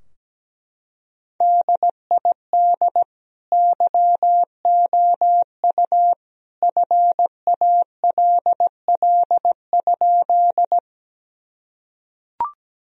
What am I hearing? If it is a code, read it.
Morse code, heard as a keyed tone, about 17 wpm, 700 Hz. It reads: DID YOU FALL?
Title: Sentences from Top 1000 Words - Review Part 2 - 17wpm